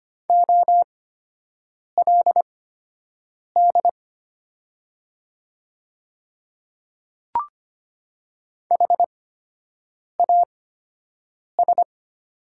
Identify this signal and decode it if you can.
Morse code, heard as a keyed tone, about 25 wpm, spaced as for 7 wpm, 700 Hz. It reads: OLD HAS
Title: 8x Character Spacing - Top 200 Words - 25wpm